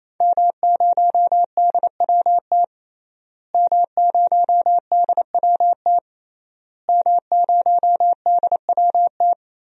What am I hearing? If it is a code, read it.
Morse code, heard as a keyed tone, about 28 wpm, 700 Hz. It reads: M0DWT M0DWT M0DWT
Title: DXCC-100 Call Signs - 5 Characters - Repeated 3x - 28wpm